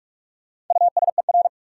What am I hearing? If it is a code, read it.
Morse code, heard as a keyed tone, about 42 wpm, 700 Hz. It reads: USER
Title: Sentences from Top 1000 Words - Review Part 5 - New Word Drill - 45wpm